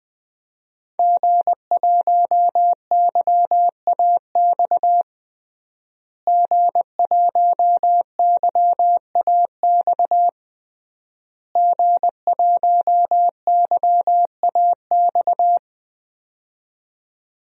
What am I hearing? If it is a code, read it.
Morse code, heard as a keyed tone, about 21 wpm, 700 Hz. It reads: G1YAX G1YAX G1YAX
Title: DXCC-100 Call Signs - 5 Characters - Repeated 3x - 20wpm